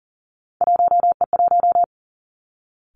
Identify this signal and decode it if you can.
Morse code, heard as a keyed tone, about 41 wpm, 700 Hz. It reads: 1E1
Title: Three Challenging Letter-Number Combinations 40wpm